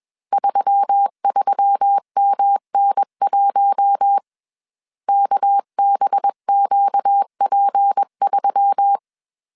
Morse code, heard as a keyed tone, about 21 wpm, 800 Hz. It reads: 33MN1 KBQP3
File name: CW / DET / 1 6 1500C S B 20wpm 2048